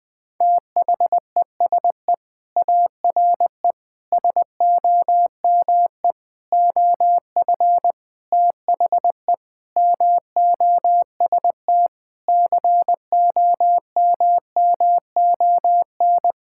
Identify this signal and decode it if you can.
Morse code, heard as a keyed tone, about 20 wpm, 700 Hz. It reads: THESE ARE SOME OF THE MOST COMMON
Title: Sentences with 4-Word Phrases - Spoken Beginning - 20wpm